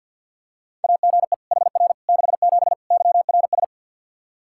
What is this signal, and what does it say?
Morse code, heard as a keyed tone, about 52 wpm, 700 Hz. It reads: AGE HR 67 YRS
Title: K1USN SST QSO Practice 50wpm